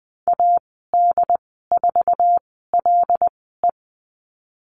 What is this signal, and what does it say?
Morse code, heard as a keyed tone, about 20 wpm, spaced as for 14 wpm, 700 Hz. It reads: AD4LE
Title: 088 Lesson 15g - Number 5 - Call Signs 20wpm